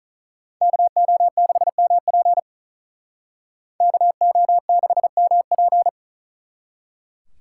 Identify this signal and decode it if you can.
Morse code, heard as a keyed tone, about 43 wpm, 700 Hz. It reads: KO6MP KO6MP
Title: Speed Racing - USA 2x2 Call Signs 35wpm